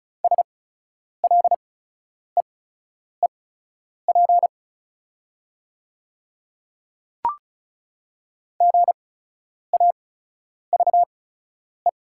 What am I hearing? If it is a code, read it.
Morse code, heard as a keyed tone, about 36 wpm, spaced as for 10 wpm, 690 Hz. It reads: SLEEP GAVE
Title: 8x Character Spacing - Top 500 Words - 35wpm